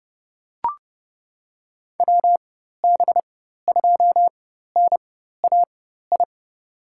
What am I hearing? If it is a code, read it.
Morse code, heard as a keyed tone, about 30 wpm, spaced as for 14 wpm, 700 Hz. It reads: WB2NAI
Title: USA Call Signs - 4x Character Spacing - 30wpm